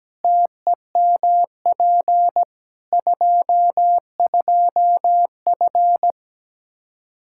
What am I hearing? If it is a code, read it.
Morse code, heard as a keyed tone, about 17 wpm, 700 Hz. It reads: TEMP 22F
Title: Practice QSO - Exchange 2 - Info - 17wpm